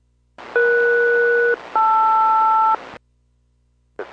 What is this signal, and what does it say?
Aircraft SELCAL, heard as two pairs of tones. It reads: ES-LR